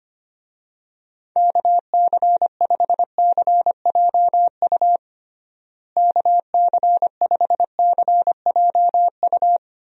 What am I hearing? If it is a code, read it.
Morse code, heard as a keyed tone, about 25 wpm, 700 Hz. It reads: KC5CJU KC5CJU
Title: USA Call Signs - 6 Characters - Repeated 3x - 25wpm